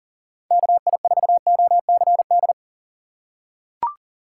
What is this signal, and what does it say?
Morse code, heard as a keyed tone, about 40 wpm, 700 Hz. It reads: KI4OCD
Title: USA Call Signs - 6 Characters - 6x Character Spacing - 40wpm